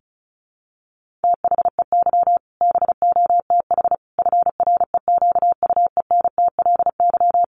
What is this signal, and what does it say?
Morse code, heard as a keyed tone, about 34 wpm, 700 Hz. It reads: THEY BOTH FREQUENTLY
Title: Sentences from Top 1300 Words 35wpm